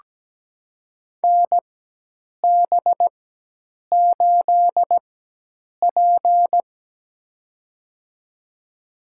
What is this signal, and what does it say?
Morse code, heard as a keyed tone, about 17 wpm, spaced as for 8 wpm, 700 Hz. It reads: NB8P
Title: USA Call Signs - 4 Characters - 4x Character Spacing - 17wpm